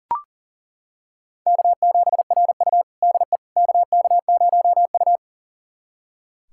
Morse code, heard as a keyed tone, about 40 wpm, 700 Hz. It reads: K7RU DE KK0U
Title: USA Callsign DE Callsign - 4 Characters - 40wpm